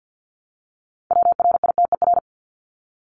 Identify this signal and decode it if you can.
Morse code, heard as a keyed tone, about 50 wpm, 700 Hz. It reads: WRITER
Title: Top 1751-2000 Words 50wpm